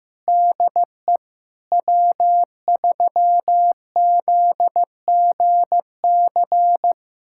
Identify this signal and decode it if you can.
Morse code, heard as a keyed tone, about 16 wpm, 700 Hz. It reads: DE W3ZGC
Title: USA Callsign DE Callsign 15wpm